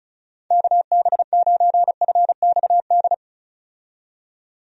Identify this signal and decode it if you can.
Morse code, heard as a keyed tone, about 34 wpm, 700 Hz. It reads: KB9FXD
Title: USA Call Signs - 6 Characters - 35wpm